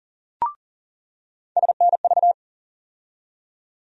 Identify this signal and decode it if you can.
Morse code, heard as a keyed tone, about 40 wpm, 700 Hz. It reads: SNV
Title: USA Call Sign Postfixes - 3 Letters 40wpm